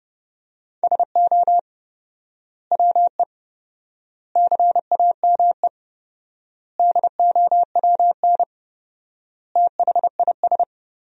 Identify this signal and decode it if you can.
Morse code, heard as a keyed tone, about 32 wpm, 700 Hz. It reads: SO WE CAME DOWN THIS
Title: 4x Word-Spacing - Sentences from Top 400 Words - 30wpm